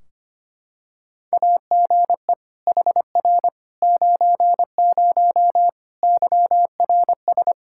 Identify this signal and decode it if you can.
Morse code, heard as a keyed tone, about 25 wpm, 700 Hz. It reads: AGE HR 90 YRS
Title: Practice QSO - Exchange 3 - More Info - 25wpm